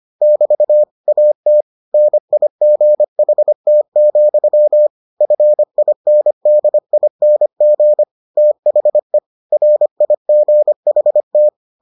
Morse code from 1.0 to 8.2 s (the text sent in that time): AT NIGHT, FINDING